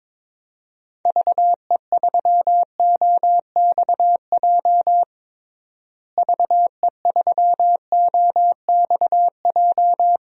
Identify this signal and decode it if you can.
Morse code, heard as a keyed tone, about 22 wpm, 700 Hz. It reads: VE3OXJ VE3OXJ
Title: DXCC-100 Call Signs - 6 Characters - Repeated 3x - 22wpm